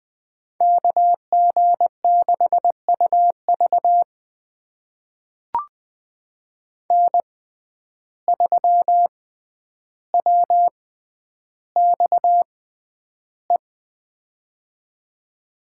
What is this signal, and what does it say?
Morse code, heard as a keyed tone, about 20 wpm, spaced as for 10 wpm, 700 Hz. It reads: KG6UV N3WXE
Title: USA Call Signs - 5 Characters - 6x Character Spacing - 20wpm